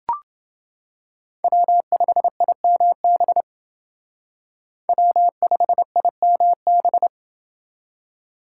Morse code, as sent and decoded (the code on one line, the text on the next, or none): .-- ..... .. -- -... / .-- ..... .. -- -...
W5IMB W5IMB